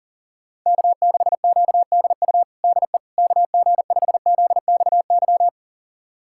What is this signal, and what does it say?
Morse code, heard as a keyed tone, about 40 wpm, 700 Hz. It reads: K6QDU DE KG5ZXY
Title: USA Callsign DE Callsign 40wpm